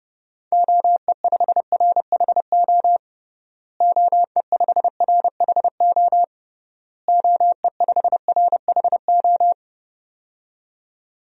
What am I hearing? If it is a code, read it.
Morse code, heard as a keyed tone, about 30 wpm, 700 Hz. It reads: OE5RHO OE5RHO OE5RHO
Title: DXCC-100 Call Signs - 6 Characters - Repeated 3x - 30wpm